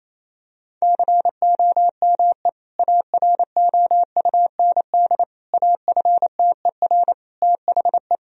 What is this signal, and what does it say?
Morse code, heard as a keyed tone, about 28 wpm, 700 Hz. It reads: COME AROUND AFTER THE
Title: Sentences from Top 1000 Words - Review Part 3 - 28wpm